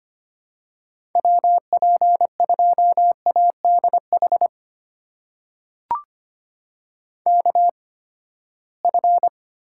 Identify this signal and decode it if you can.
Morse code, heard as a keyed tone, about 25 wpm, spaced as for 17 wpm, 700 Hz. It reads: WP2ADH KF
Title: USA Call Signs - 6 Characters - 8x Character Spacing - 25wpm